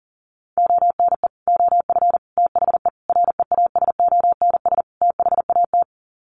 Morse code, heard as a keyed tone, about 40 wpm, 700 Hz. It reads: ONE OF THE REASONS THAT
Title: Sets of 5 Words Encore 40wpm